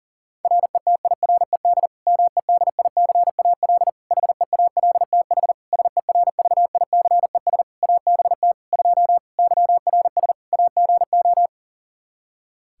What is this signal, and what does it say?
Morse code, heard as a keyed tone, about 40 wpm, 700 Hz. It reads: RETIRED MEDICAL HEALTH SERVICES ABT 2 YRS AGO
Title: Practice QSO - Exchange 3 - More Info - 40wpm